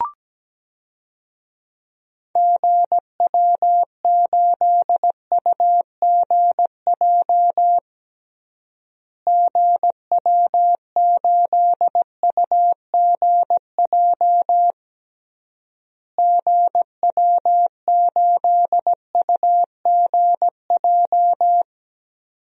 Morse code, heard as a keyed tone, about 17 wpm, 700 Hz. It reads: GW8UGJ GW8UGJ GW8UGJ
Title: DXCC-100 Call Signs - 6 Characters - Repeated 3x - 17wpm